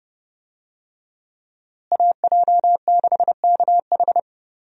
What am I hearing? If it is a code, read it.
Morse code, heard as a keyed tone, about 30 wpm, 700 Hz. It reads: AJ6KH
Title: USA Call Signs - 5 Characters - Repeated 3x - 30wpm